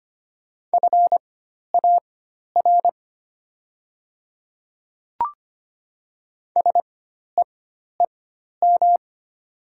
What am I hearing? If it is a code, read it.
Morse code, heard as a keyed tone, about 25 wpm, spaced as for 12 wpm, 700 Hz. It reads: FAR SEEM
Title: 4x Character Spacing - Top 300 Words - 25wpm